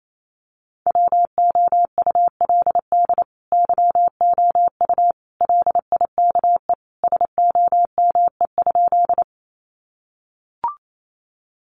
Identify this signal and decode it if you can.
Morse code, heard as a keyed tone, about 28 wpm, 700 Hz. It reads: WOULD YOU LIKE SOME?
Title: Sentences from Top 100 Words 28wpm